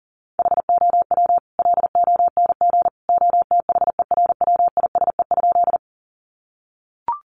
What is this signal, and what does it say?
Morse code, heard as a keyed tone, about 40 wpm, 700 Hz. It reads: HOW LONG OTHERWISE?
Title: Sentences from Top 1200 Words 40wpm